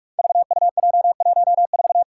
Morse code, heard as a keyed tone, about 44 wpm, 700 Hz. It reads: VUJ14
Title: Five Challenging Letter-Number Combinations 45wpm